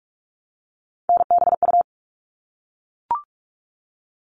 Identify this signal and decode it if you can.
Morse code, heard as a keyed tone, about 44 wpm, 700 Hz. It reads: NBU
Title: All Cut Numbers - 3-Digit 45wpm